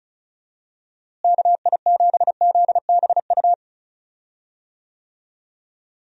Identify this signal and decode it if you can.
Morse code, heard as a keyed tone, about 36 wpm, 700 Hz. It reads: KI7ZBU
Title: USA Call Signs - 6 Characters - Repeated 3x - 35wpm